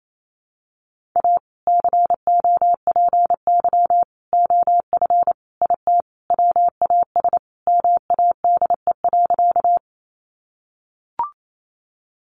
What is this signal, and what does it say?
Morse code, heard as a keyed tone, about 28 wpm, 700 Hz. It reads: A COPY OF IT WAS MADE.
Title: Sentences from Top 1400 Words 28wpm